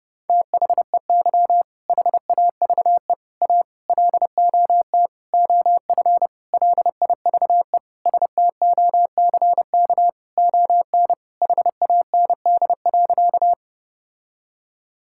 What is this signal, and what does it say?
Morse code, heard as a keyed tone, about 30 wpm, 700 Hz. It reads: THEY HAVE A LOT OF LIVE STOCK ON HAND.
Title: Sentences from Top 1000 Words - Review Part 5 - 30wpm